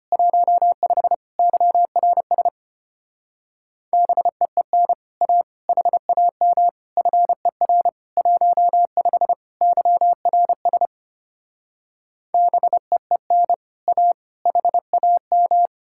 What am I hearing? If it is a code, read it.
Morse code, heard as a keyed tone, about 33 wpm, 700 Hz. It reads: 15 YRS BEEN A HAM FER 15 YRS BEEN A HAM